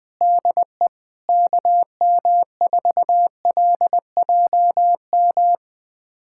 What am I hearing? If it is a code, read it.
Morse code, heard as a keyed tone, about 20 wpm, 700 Hz. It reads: DE KM4LJM
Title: USA Callsign DE Callsign - 6 Characters - 20wpm